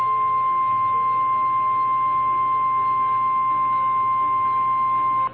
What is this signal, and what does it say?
Emergency Alert System (SAME) tones (the 1050 Hz attention tone). It is heard from 0.5 to 4.7 s.
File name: eas / KNCYKBIE / KNCYKBIE